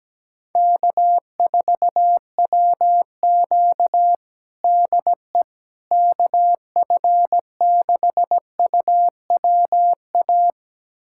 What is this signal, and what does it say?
Morse code, heard as a keyed tone, about 17 wpm, 700 Hz. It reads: K4WQ DE KF6UWA
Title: USA Callsign DE Callsign 17wpm